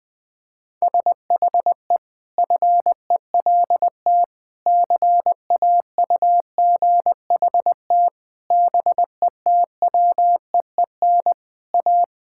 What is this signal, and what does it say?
Morse code, heard as a keyed tone, about 20 wpm, 700 Hz. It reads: SHE FELT CAUGHT BETWEEN A
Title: Sentences with 8-Word Phrases - 20wpm